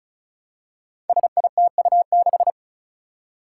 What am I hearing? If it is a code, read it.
Morse code, heard as a keyed tone, about 36 wpm, 700 Hz. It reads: SITU6
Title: Five Challenging Letter-Number Combinations 35wpm